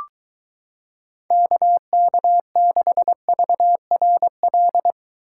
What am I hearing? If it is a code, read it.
Morse code, heard as a keyed tone, about 23 wpm, 700 Hz. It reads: KK6VRL